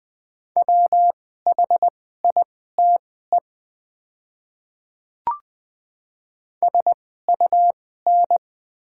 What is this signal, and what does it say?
Morse code, heard as a keyed tone, about 20 wpm, spaced as for 15 wpm, 690 Hz. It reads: WHITE SUN
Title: 2x Character Spacing - Top 500 Words - 20wpm